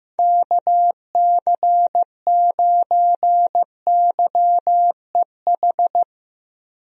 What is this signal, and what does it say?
Morse code, heard as a keyed tone, about 15 wpm, 700 Hz. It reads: KC9YEH
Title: USA Call Signs - 6 Characters - Repeated 2x - 15wpm